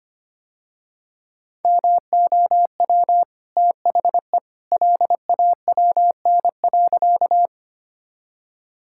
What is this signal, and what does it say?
Morse code, heard as a keyed tone, about 25 wpm, 700 Hz. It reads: MOW THE LAWN.
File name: Easy Sentences with Hints - 25wpm